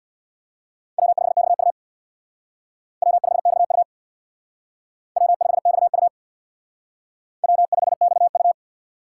Morse code, heard as a keyed tone, about 63 wpm, 700 Hz. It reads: W5XU W5XU W5XU W5XU